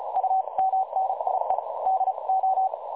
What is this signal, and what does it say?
Morse code, heard as a keyed tone, about 33 wpm, 760 Hz, with noise. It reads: U MNIE DO
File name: dlazaawansowanych / nagraniazpasma / websdr recording 3565.0kHz